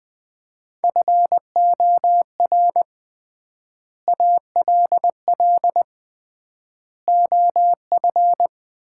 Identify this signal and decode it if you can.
Morse code, heard as a keyed tone, about 20 wpm, 700 Hz. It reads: FOR ALL OF
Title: 3x Word-Spacing - Sets of 5 Words - 20wpm